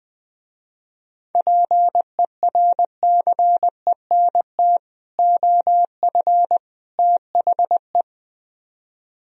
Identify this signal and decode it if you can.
Morse code, heard as a keyed tone, about 21 wpm, 700 Hz. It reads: PERCENT OF THE
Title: Speed-Racing - Sets of 3 Words 17wpm